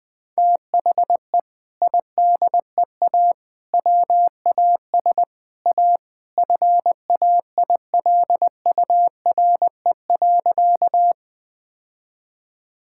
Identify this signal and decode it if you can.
Morse code, heard as a keyed tone, about 21 wpm, 700 Hz. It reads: THE IDEA WAS A FAILURE.
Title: Sentences from Top 1000 Words - Review Part 6 - 20wpm